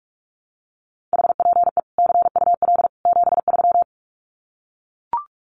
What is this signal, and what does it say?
Morse code, heard as a keyed tone, about 44 wpm, 700 Hz. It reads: HPE CUL 73
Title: Common QSO Phrases 45wpm